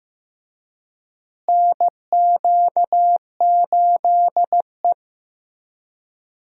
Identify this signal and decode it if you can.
Morse code, heard as a keyed tone, about 16 wpm, 700 Hz. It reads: NQ8E
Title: USA Call Signs - 4 Characters - Repeated 2x - 15wpm